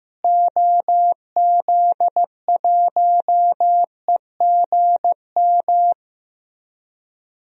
Morse code, heard as a keyed tone, about 15 wpm, 700 Hz. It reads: OZ1EGM
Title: DXCC-100 Call Signs - 6 Characters - Repeated 3x - 15wpm